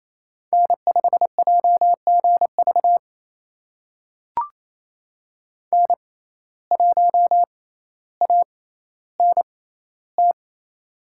Morse code, heard as a keyed tone, about 28 wpm, spaced as for 14 wpm, 700 Hz. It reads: N5JGV N1ANT
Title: USA Call Signs - 6x Character Spacing - 28wpm